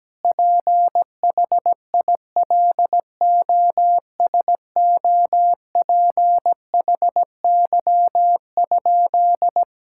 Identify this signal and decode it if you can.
Morse code, heard as a keyed tone, about 17 wpm, 700 Hz. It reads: PHILOSOPHY?
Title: Sentences from Top 1000 Words - Review Part 6 - 17wpm